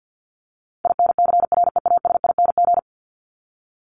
Morse code, heard as a keyed tone, about 49 wpm, 700 Hz. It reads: INCREASING
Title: ING Suffix - Top 100 Words with Base Words 50wpm